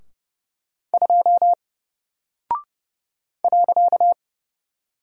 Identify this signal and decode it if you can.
Morse code, heard as a keyed tone, about 30 wpm, 700 Hz. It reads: 2 .
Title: Single Character (Letter:Number:Punctuation:Prosign) 30wpm